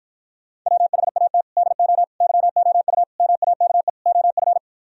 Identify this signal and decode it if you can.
Morse code, heard as a keyed tone, about 52 wpm, 700 Hz. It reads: WHAT DO YOU MAKE OF